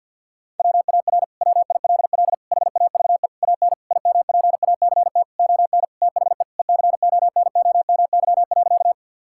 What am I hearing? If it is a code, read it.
Morse code, heard as a keyed tone, about 52 wpm, 700 Hz. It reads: WAR WILL HAVE AN IMPACT ON THE ECONOMY.
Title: Sentences from Top 1000 Words - Review Part 6 - 50wpm